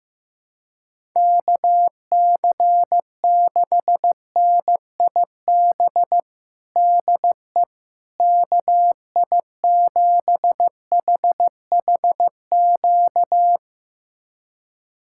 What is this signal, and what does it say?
Morse code, heard as a keyed tone, about 15 wpm, 700 Hz. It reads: KC6NIB DE KI7HHQ
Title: USA Callsign DE Callsign - 6 Characters - 15wpm